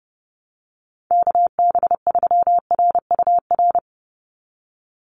Morse code, heard as a keyed tone, about 30 wpm, 700 Hz. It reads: KB3RUR
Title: USA Call Signs 30wpm